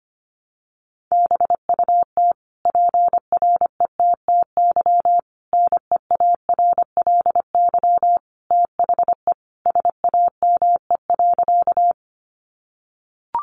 Morse code, heard as a keyed tone, about 25 wpm, 700 Hz. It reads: BUT PRETTY NEARLY THE SAME.
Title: Sentences from Top 1300 Words 25wpm